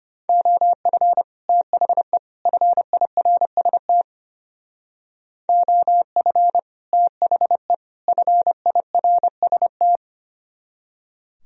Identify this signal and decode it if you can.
Morse code, heard as a keyed tone, about 30 wpm, 700 Hz. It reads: OF THE FIRST OF THE FIRST
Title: Speed-Racing - Sets of 3 Words 25wpm